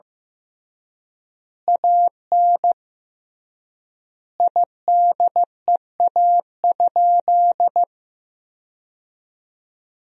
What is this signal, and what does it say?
Morse code, heard as a keyed tone, about 15 wpm, 700 Hz. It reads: AN IDEA?
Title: 3x Word-Spacing - Sentences from Top 400 Words - 15wpm